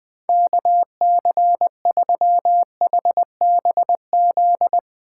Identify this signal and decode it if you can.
Morse code, heard as a keyed tone, about 20 wpm, 700 Hz. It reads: KC3HBZ
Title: Speed Racing - USA Call Signs 20wpm